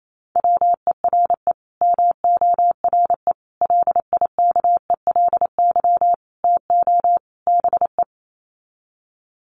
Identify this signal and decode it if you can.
Morse code, heard as a keyed tone, about 28 wpm, 700 Hz. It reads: WERE MORE LIKELY TO BE
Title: Sets of 5 Words Encore2 28wpm